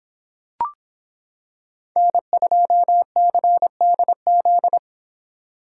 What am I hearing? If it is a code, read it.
Morse code, heard as a keyed tone, about 25 wpm, 700 Hz. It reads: N2CDZ